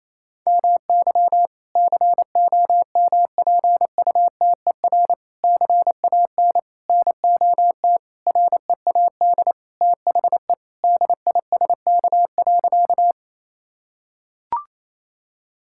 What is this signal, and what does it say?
Morse code, heard as a keyed tone, about 28 wpm, 700 Hz. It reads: MY COMPUTER CAN NOT READ THE DISK.
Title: Sentences from Top 1000 Words - Review Part 6 - 28wpm